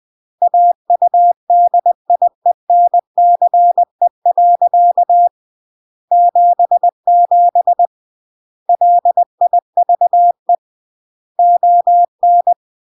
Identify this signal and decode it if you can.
Morse code, heard as a keyed tone, about 20 wpm, 700 Hz. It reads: AUDIENCE. 77 LIVE ON
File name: CW / jodorowsky / 82 commandments 20-20